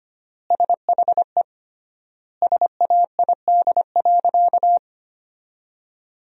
Morse code, heard as a keyed tone, about 25 wpm, 700 Hz. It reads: SHE SAID.
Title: 3x Word-Spacing - Sentences from Top 500 Words - 25wpm